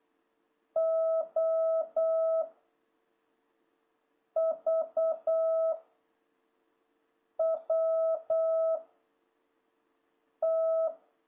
Morse code, heard as a keyed tone, about 8 wpm, 650 Hz. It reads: O V W T
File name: jh8cbh / 2026cw